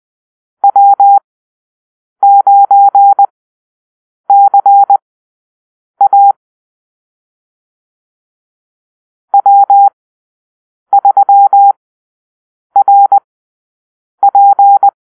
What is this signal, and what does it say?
Morse code, heard as a keyed tone, about 20 wpm, spaced as for 7 wpm, 800 Hz. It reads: W9CA W3RP
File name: arrl / Call-Sign Drill-7